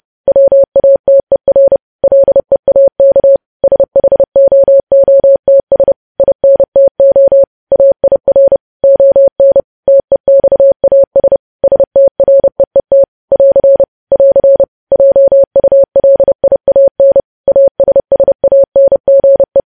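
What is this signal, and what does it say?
Morse code, heard as a keyed tone, about 30 wpm, 550 Hz. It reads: WATER LEAK SHOOTS INTO AIR ON TEXAS STREET <AR> <AR> JULIAN ASSANGE
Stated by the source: Morse code news headlines practice, preceded by vvv.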